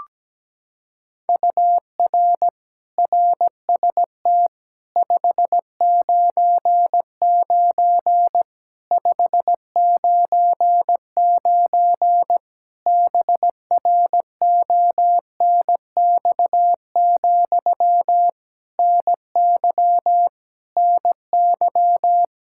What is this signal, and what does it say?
Morse code, heard as a keyed tone, about 17 wpm, 700 Hz. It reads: UR RST 599 599 BRONX, NY NY
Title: Practice QSO - Exchange 1 - Beginning - 17wpm